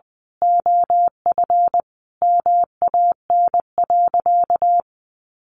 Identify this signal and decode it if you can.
Morse code, heard as a keyed tone, about 20 wpm, 700 Hz. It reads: OF MAN.
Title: Sentences from Top 1400 Words 20wpm